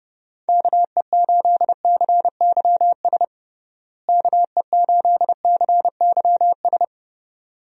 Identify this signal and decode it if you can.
Morse code, heard as a keyed tone, about 30 wpm, 700 Hz. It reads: KE8CYS KE8CYS
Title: USA Call Signs - 6 Characters - Repeated 2x - 30wpm